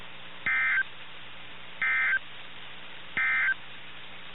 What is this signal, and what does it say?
Emergency Alert System (SAME) tones, heard as harsh buzzing bursts. It is heard from 0.3 to 3.6 s.
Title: NOAA Weather Radio - KIH61